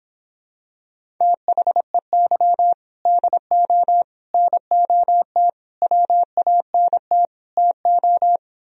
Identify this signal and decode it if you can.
Morse code, heard as a keyed tone, about 26 wpm, 700 Hz. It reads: THEY DO NOT WANT TO